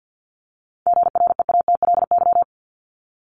Morse code, heard as a keyed tone, about 50 wpm, 700 Hz. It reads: GREATLY
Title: Top 1251-1500 Words 50wpm